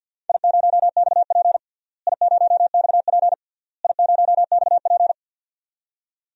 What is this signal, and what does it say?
Morse code, heard as a keyed tone, about 47 wpm, 700 Hz. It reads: I0XP I0XP I0XP
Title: DXCC-100 Call Signs - 4 Characters - Repeated 3x - 50wpm